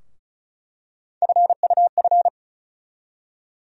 Morse code, heard as a keyed tone, about 35 wpm, 700 Hz. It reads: FUF
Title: USA Call Sign Postfixes - 3 Letters 35wpm